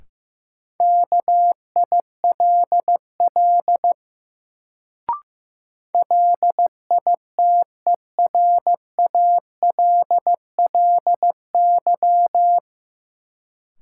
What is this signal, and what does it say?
Morse code, heard as a keyed tone, about 15 wpm, 700 Hz. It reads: KILL LITERALLY
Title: Sentences from Top 1400 Words - New Word Drill - 15wpm